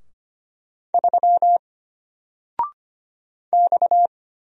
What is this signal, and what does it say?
Morse code, heard as a keyed tone, about 25 wpm, 700 Hz. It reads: 3 X
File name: Single-Letter-Number-25wpm